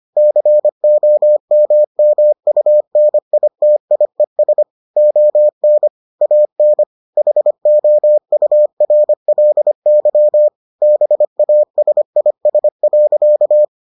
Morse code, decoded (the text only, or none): COMMUNITIES ON AN HOURLY BASIS.